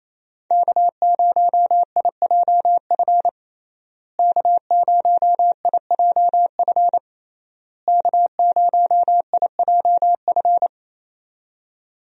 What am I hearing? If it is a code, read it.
Morse code, heard as a keyed tone, about 28 wpm, 700 Hz. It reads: K0IJF K0IJF K0IJF
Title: USA Call Signs - 5 Characters - Repeated 3x - 28wpm